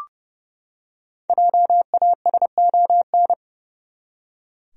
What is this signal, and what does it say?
Morse code, heard as a keyed tone, about 30 wpm, 700 Hz. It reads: JASON
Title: USA Common Names - Max 5 Letters - 30wpm